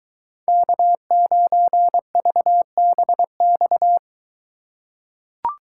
Morse code, heard as a keyed tone, about 21 wpm, 700 Hz. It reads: K9VBX